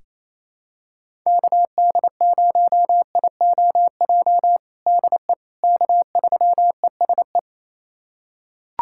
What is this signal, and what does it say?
Morse code, heard as a keyed tone, about 28 wpm, 700 Hz. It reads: KD0IOJ DE K3ESE
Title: USA Callsign DE Callsign 28wpm